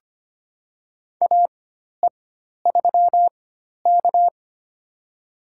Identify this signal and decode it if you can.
Morse code, heard as a keyed tone, about 25 wpm, spaced as for 12 wpm, 700 Hz. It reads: AE3K
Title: USA Call Signs - 4x Character Spacing - 25wpm